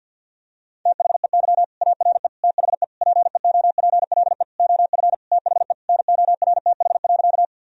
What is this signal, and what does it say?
Morse code, heard as a keyed tone, about 51 wpm, 700 Hz. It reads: THEY ARE THE PEOPLE OF THE NORTH.
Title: Sentences from Top 1000 Words - Review Part 1 - 50wpm